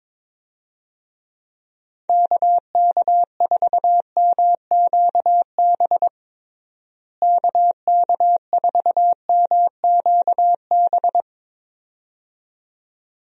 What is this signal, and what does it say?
Morse code, heard as a keyed tone, about 22 wpm, 700 Hz. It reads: KK4MQB KK4MQB
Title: USA Call Signs - 6 Characters - Repeated 2x - 22wpm